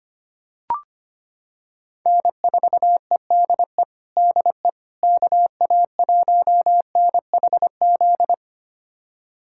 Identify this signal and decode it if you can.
Morse code, heard as a keyed tone, about 25 wpm, 700 Hz. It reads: N4EDE DE KA1NHZ
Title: USA Callsign DE Callsign 25wpm